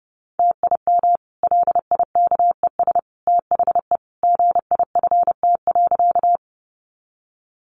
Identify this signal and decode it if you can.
Morse code, heard as a keyed tone, about 30 wpm, 700 Hz. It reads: TIM LIKES THE GIFT.
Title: Easy Phrases 30wpm